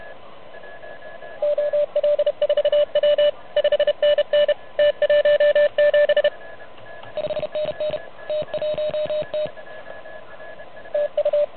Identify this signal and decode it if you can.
Morse code, heard as a keyed tone, about 32 wpm, 590 Hz, with radio noise. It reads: OL4W 5NN T17 5NN T1T TU
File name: truSDX / MWC29.8